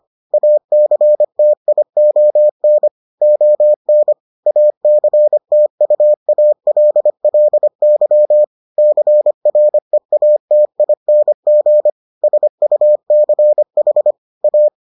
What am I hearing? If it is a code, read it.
Morse code, heard as a keyed tone, about 25 wpm, 600 Hz. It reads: ACTION ON ACTUALLY CREATING SUCH A